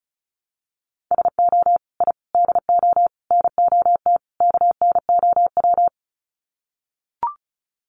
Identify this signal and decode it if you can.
Morse code, heard as a keyed tone, about 34 wpm, 700 Hz. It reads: SO I DO NOT KNOW
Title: Sets of 5 Words Encore 35wpm